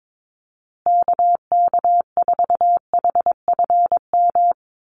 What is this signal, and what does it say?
Morse code, heard as a keyed tone, about 22 wpm, 700 Hz. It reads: KK4HFM
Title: USA Call Signs Repeated 3x 22wpm